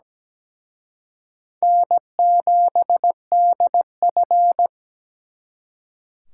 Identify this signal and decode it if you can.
Morse code, heard as a keyed tone, about 20 wpm, 700 Hz. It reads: N7DF
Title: Speed Racing - USA 1x2 Call Signs 17wpm